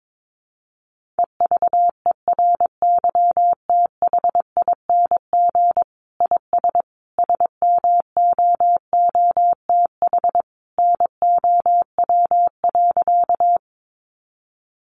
Morse code, heard as a keyed tone, about 22 wpm, 700 Hz. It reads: EVERYTHING IS SMOOTH NOW.
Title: Sentences from Top 1400 Words 22wpm